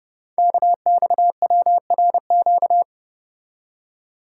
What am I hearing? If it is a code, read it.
Morse code, heard as a keyed tone, about 30 wpm, 700 Hz. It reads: KXWRQ
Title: Code Groups - 5 Letters - 30wpm